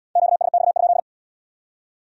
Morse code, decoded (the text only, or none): QSX?